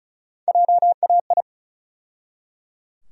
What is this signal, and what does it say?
Morse code, heard as a keyed tone, about 34 wpm, 700 Hz. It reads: JAI
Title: USA Call Sign Postfixes - 3 Letters 35wpm